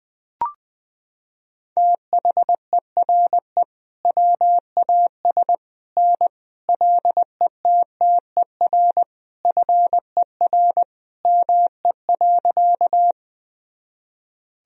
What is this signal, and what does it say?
Morse code, heard as a keyed tone, about 20 wpm, 700 Hz. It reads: THERE WAS N LETTER FER ME.
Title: Sentences from Top 300 Words and Abbreviations 20wpm